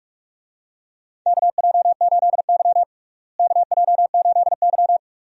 Morse code, heard as a keyed tone, about 46 wpm, 700 Hz. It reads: KJ8Y KJ8Y
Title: USA Call Signs - 4 Characters - Repeated 2x - 45wpm